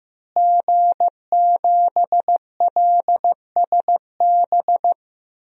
Morse code, heard as a keyed tone, about 16 wpm, 700 Hz. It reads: G7LSB
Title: DXCC-100 Call Signs - 5 Characters - Repeated 2x - 15wpm